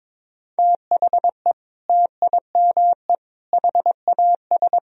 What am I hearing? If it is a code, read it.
Morse code, heard as a keyed tone, about 22 wpm, 700 Hz. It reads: THE TIME HAS
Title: Sentences from Top 1000 Words - Review Part 1 - 22wpm